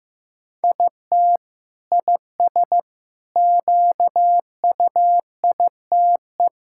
Morse code, heard as a keyed tone, about 15 wpm, 700 Hz. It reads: IT IS QUITE
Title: Sentences from Top 1000 Words - Review Part 5 - 15wpm